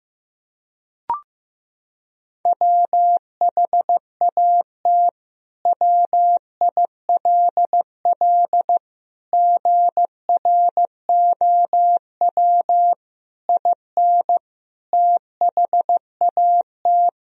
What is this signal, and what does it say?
Morse code, heard as a keyed tone, about 15 wpm, 700 Hz. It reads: WHAT WILL GROW IN THAT
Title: Sentences from Top 1000 Words - Review Part 2 - 15wpm